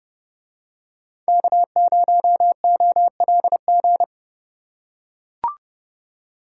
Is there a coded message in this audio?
Yes: Morse code from 1.3 to 4.0 s, heard as a keyed tone, about 28 wpm, 700 Hz. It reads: K0OLG